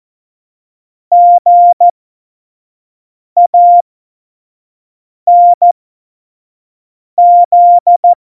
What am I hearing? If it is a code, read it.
Morse code, heard as a keyed tone, about 14 wpm, spaced as for 5 wpm, 700 Hz. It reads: GANZ
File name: Morsekurs / Witze / Witz 08. August 17, Tempo 25 F70